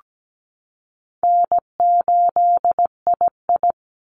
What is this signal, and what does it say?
Morse code, heard as a keyed tone, about 17 wpm, 700 Hz. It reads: N8II
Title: Easy Call Signs Repeated 2x - 17wpm